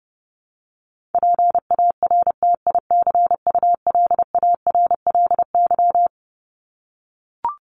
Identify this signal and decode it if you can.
Morse code, heard as a keyed tone, about 30 wpm, 700 Hz. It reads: PARTICULARLY
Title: LY Suffix - Top 100 Words with Base Words 30wpm